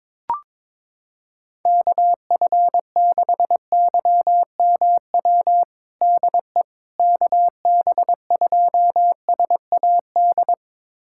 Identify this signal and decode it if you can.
Morse code, heard as a keyed tone, about 21 wpm, 700 Hz. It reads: KF6YMW DE KB2SAD
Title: USA Callsign DE Callsign 22wpm